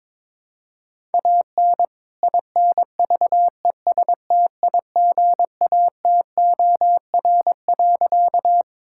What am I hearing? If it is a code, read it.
Morse code, heard as a keyed tone, about 21 wpm, 700 Hz. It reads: AN INVESTIGATOR.
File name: All-Age Appropriate Jokes - 22wpm